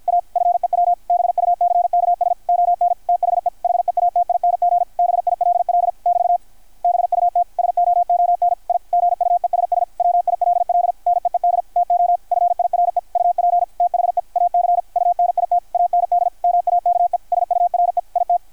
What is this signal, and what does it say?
Morse code, heard as a keyed tone, about 53 wpm, 700 Hz, with radio noise. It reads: A FEW BUCKS ON THE HEATING BILL <BT> BUT SOON I GUESS WILL NEED TO FIRE UP THE AC UNIT AND MAKE SURE IT
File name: archived / CRSnet